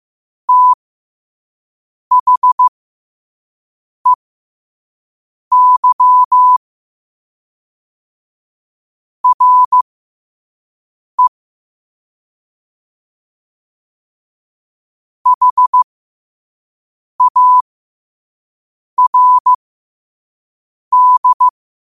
Morse code, heard as a keyed tone, about 15 wpm, spaced as for 5 wpm, 1000 Hz. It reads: THEY RE HARD
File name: qotdfiles / 5wpm